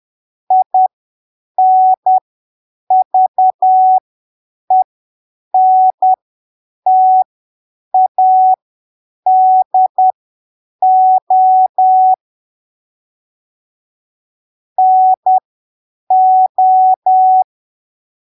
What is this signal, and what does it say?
Morse code, heard as a keyed tone, about 10 wpm, spaced as for 7 wpm, 750 Hz. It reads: INVENTADO NO